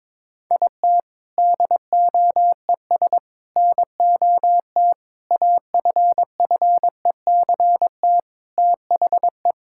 Morse code, heard as a keyed tone, about 21 wpm, 700 Hz. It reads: IT DOES NOT AFFECT THE
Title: Sentences from Top 1000 Words - Review Part 6 - 22wpm